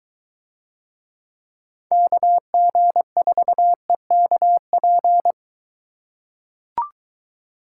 Morse code, heard as a keyed tone, about 22 wpm, 700 Hz. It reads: KG4EKP